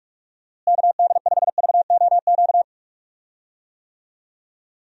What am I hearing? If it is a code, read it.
Morse code, heard as a keyed tone, about 46 wpm, 700 Hz. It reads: KD5VOQ
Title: USA Call Signs - 6 Characters - Repeated 2x - 45wpm